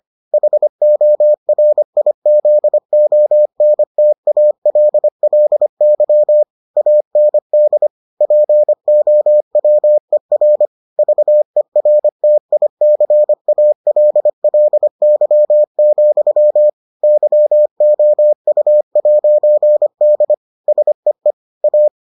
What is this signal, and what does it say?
Morse code, heard as a keyed tone, about 25 wpm, 600 Hz. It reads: HORIZONTALLY AND POWER VERTICALLY, YOU'D SEE A